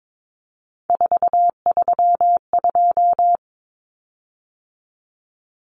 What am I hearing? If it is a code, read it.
Morse code, heard as a keyed tone, about 22 wpm, 700 Hz. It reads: 432
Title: Common Cut Numbers - 3-Digit 22wpm